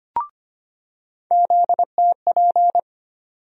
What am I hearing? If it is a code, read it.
Morse code, heard as a keyed tone, about 25 wpm, 700 Hz. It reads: ZTP
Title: USA Call Sign Postfixes - 3 Letters 25wpm